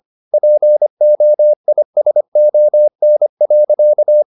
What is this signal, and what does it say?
Morse code, heard as a keyed tone, about 25 wpm, 600 Hz. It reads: POISON.
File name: the old man and the sea part 1 25wpm